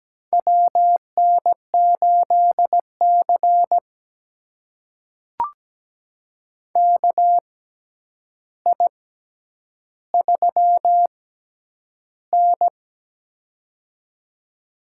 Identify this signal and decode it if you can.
Morse code, heard as a keyed tone, about 17 wpm, spaced as for 9 wpm, 700 Hz. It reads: WN8C KI3N
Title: USA Call Signs - 4 Characters - 6x Character Spacing - 17wpm